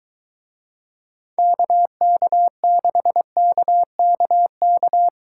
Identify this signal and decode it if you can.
Morse code, heard as a keyed tone, about 23 wpm, 700 Hz. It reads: KK6KKK